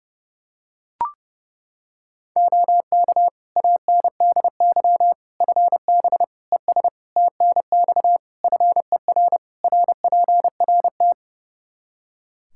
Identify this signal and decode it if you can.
Morse code, heard as a keyed tone, about 30 wpm, 700 Hz. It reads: OK ANDY FB ES TNX FER RPRT
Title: K1USN SST QSO Practice 30wpm